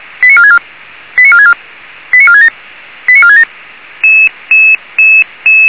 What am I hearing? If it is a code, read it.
Five-tone selective call (ZVEI), heard as quick runs of five tones, tones of about 80 ms. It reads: 89464 89464 89467 89467
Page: /464/SDR/ZVEI/